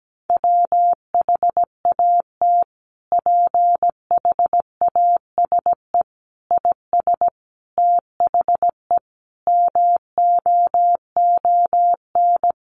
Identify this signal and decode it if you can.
Morse code, heard as a keyed tone, about 17 wpm, 700 Hz. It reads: WHAT PHASE IS THE MOON
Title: Sentences from Top 1200 Words 17wpm